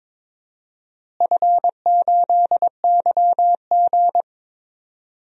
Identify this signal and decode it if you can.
Morse code, heard as a keyed tone, about 22 wpm, 700 Hz. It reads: F8YG
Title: DXCC-100 Call Signs - 4 Characters - Repeated 2x - 22wpm